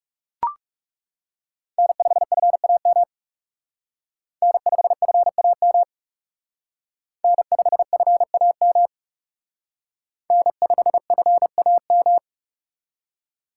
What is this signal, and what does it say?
Morse code, heard as a keyed tone, about 37 wpm, 700 Hz. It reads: N5FAM N5FAM N5FAM N5FAM
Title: Speed Racing - USA 1x3 Call Signs 30wpm